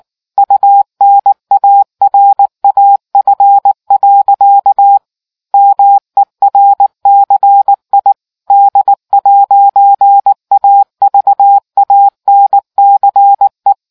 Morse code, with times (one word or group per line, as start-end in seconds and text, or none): 0.4-5.0 UNARAF.
5.5-8.1 MERCI
8.5-13.7 D'AVANCE